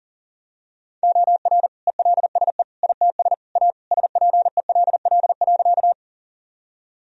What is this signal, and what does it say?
Morse code, heard as a keyed tone, about 40 wpm, 700 Hz. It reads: OR ELSE ITS A SPELL.
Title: Sentences from Top 1000 Words - Review Part 1 - 40wpm